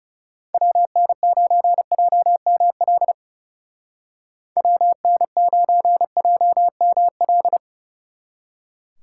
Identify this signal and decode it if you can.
Morse code, heard as a keyed tone, about 37 wpm, 700 Hz. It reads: WN9JML WN9JML
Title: Speed Racing - USA 2x3 Call Signs 30wpm